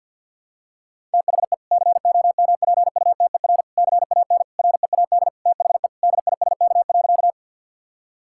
Morse code, heard as a keyed tone, about 50 wpm, 700 Hz. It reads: THE COMPUTER CAN READ THE DISK.
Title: Sentences from Top 1000 Words - Review Part 6 - 50wpm